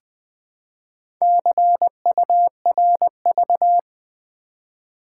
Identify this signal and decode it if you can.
Morse code, heard as a keyed tone, about 20 wpm, 690 Hz. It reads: CURV
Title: Build A Word - 2 Letters At A Time - 6-Letter Words - 20wpm